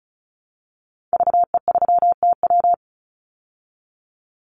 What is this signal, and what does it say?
Morse code, heard as a keyed tone, about 34 wpm, 700 Hz. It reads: VE3TW
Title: Call Signs Repeated 3x - 35wpm